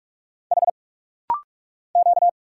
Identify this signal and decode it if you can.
Morse code, heard as a keyed tone, about 46 wpm, 690 Hz. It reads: H Q
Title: Single Letters - Rapid-Fire 45wpm